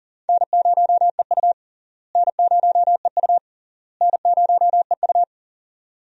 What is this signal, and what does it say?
Morse code, heard as a keyed tone, about 40 wpm, 700 Hz. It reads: N0EU N0EU N0EU
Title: USA Call Signs - 4 Characters - Repeated 3x - 40wpm